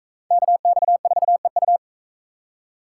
Morse code, heard as a keyed tone, about 39 wpm, 700 Hz. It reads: KX4EU